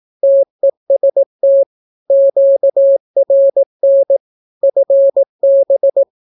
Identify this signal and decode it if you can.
Morse code, heard as a keyed tone, about 18 wpm, 550 Hz. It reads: TEST QRN FB
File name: naukapodstaw / skroty / skroty